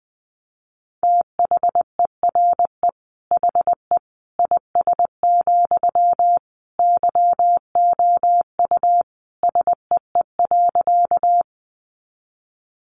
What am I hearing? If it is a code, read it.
Morse code, heard as a keyed tone, about 20 wpm, 700 Hz. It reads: THERE HE IS, YOU SEE.
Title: Sentences from Top 100 Words 20wpm